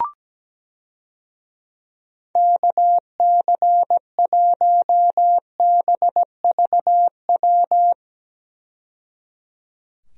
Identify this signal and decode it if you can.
Morse code, heard as a keyed tone, about 17 wpm, 700 Hz. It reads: KC1BVW
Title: USA Call Signs - 6 Characters - 17wpm